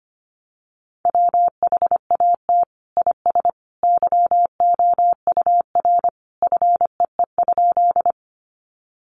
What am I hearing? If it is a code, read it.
Morse code, heard as a keyed tone, about 25 wpm, 700 Hz. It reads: WHAT IS YOUR FEE?
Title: Sentences from Top 1500 Words 25wpm